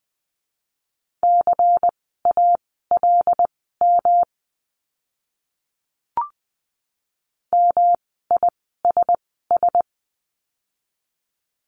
Morse code, heard as a keyed tone, about 20 wpm, spaced as for 14 wpm, 700 Hz. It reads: CALM MISS
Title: 108 Lesson 18f - Letter m - Words 20wpm